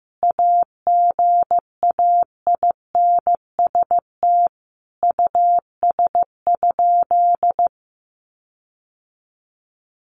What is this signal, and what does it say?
Morse code, heard as a keyed tone, about 15 wpm, 700 Hz. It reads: AGAINST US?
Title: Sentences from Top 1500 Words 15wpm